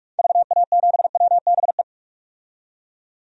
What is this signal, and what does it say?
Morse code, heard as a keyed tone, about 46 wpm, 700 Hz. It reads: VA7WBE
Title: DXCC-100 Call Signs - 6 Characters - 45wpm